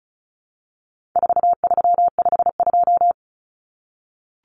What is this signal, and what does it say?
Morse code, heard as a keyed tone, about 34 wpm, 700 Hz. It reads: 4352
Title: Common Cut Numbers - 4-Digit 35wpm